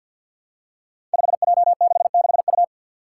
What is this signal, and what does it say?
Morse code, heard as a keyed tone, about 47 wpm, 700 Hz. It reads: 5J66U
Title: Five Challenging Letter-Number Combinations 50wpm